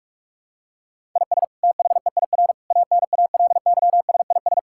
Morse code, heard as a keyed tone, about 46 wpm, 700 Hz. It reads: IS THEIR ANALYSIS
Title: Sentences from Top 1000 Words - Review Part 5 - 45wpm